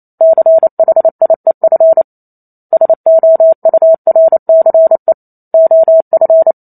Morse code, heard as a keyed tone, about 28 wpm, 650 Hz. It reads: CHIEF SOURCE OF